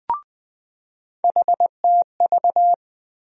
Morse code, heard as a keyed tone, about 20 wpm, 700 Hz. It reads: HTV
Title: USA Call Sign Postfixes - 3 Letters 20wpm